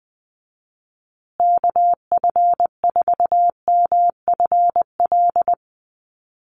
Morse code, heard as a keyed tone, about 20 wpm, 700 Hz. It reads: KF4MFL
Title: USA Call Signs Repeated 2x 20wpm